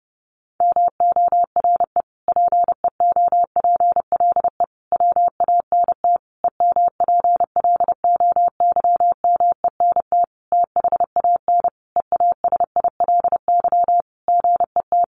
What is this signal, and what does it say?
Morse code, heard as a keyed tone, about 30 wpm, 700 Hz. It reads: MORE PEOPLE WANT EMPLOYMENT THAN EASILY GET
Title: Sentences from Top 1500 Words 30wpm